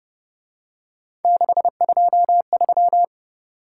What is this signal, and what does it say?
Morse code, heard as a keyed tone, about 30 wpm, 700 Hz. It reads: 623